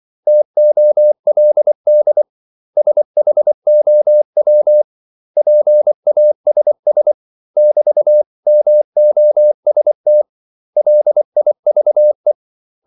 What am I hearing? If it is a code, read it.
Morse code, heard as a keyed tone, about 24 wpm, 600 Hz. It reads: TOLD SHOW PASS <BT>MOST LIVE
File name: CW / Practice / 500MostCommonEnglishWords24